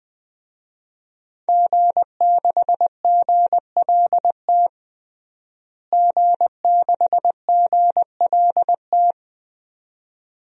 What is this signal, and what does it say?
Morse code, heard as a keyed tone, about 20 wpm, 700 Hz. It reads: G6GLT G6GLT
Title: DXCC-100 Call Signs - 5 Characters - Repeated 2x - 20wpm